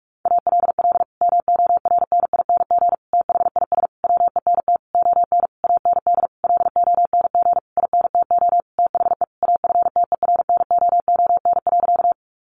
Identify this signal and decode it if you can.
Morse code, heard as a keyed tone, about 46 wpm, 700 Hz. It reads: ALL MORNING THIS WENT ON AND LONG INTO THE AFTERNOON.
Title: Sentences from Top 1500 Words 45wpm